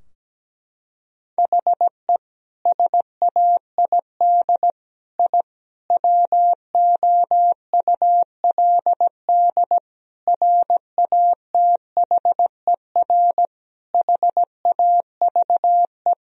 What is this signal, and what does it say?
Morse code, heard as a keyed tone, about 17 wpm, 700 Hz. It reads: HE SAID I WOULD RATHER HAVE
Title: Sentences from Top 1000 Words - Review Part 4 - 17wpm